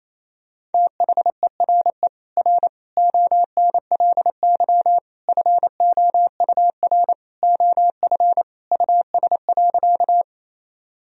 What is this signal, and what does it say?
Morse code, heard as a keyed tone, about 28 wpm, 700 Hz. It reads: THERE R ONLY FOUR OF US.
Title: Sentences from Top 300 Words and Abbreviations 28wpm